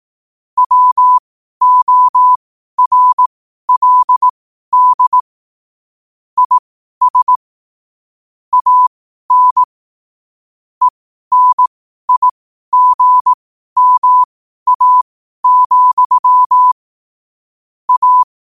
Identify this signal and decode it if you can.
Morse code, heard as a keyed tone, about 18 wpm, spaced as for 12 wpm, 1000 Hz. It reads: WORLD IS AN ENIGMA, A
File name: qotdfiles / 12wpm